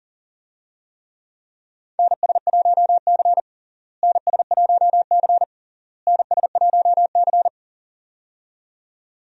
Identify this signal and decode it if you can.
Morse code, heard as a keyed tone, about 40 wpm, 700 Hz. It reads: NS1C NS1C NS1C
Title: USA Call Signs - 4 Characters - Repeated 3x - 40wpm